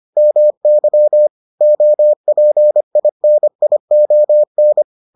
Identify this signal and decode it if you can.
Morse code, heard as a keyed tone, about 25 wpm, 600 Hz. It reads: MY OPINION